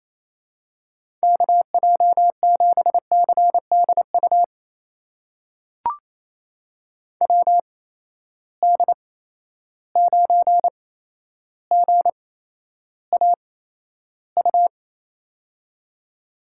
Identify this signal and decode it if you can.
Morse code, heard as a keyed tone, about 28 wpm, spaced as for 12 wpm, 700 Hz. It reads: KJ7CDU WD9GAU
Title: USA Call Signs - 6 Characters - 8x Character Spacing - 28wpm